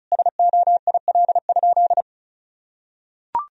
Morse code, heard as a keyed tone, about 36 wpm, 700 Hz. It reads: SOIL?
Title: Sentences from Top 1000 Words - Review Part 4 - 35wpm